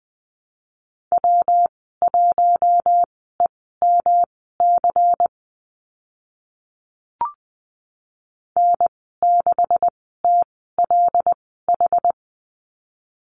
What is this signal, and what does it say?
Morse code, heard as a keyed tone, about 20 wpm, spaced as for 14 wpm, 700 Hz. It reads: W1EMC N6TLH
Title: 140 Lesson 23g - Letter f - Call Signs 20wpm